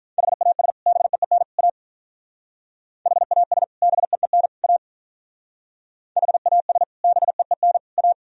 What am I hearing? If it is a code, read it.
Morse code, heard as a keyed tone, about 43 wpm, 700 Hz. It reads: HAS BEEN A HAS BEEN A HAS BEEN A
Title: Speed-Racing - Sets of 3 Words 35wpm